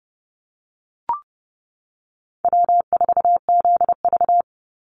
Morse code, heard as a keyed tone, about 30 wpm, 700 Hz. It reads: W4ZV
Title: NA SOTA Chaser Callsigns 30wpm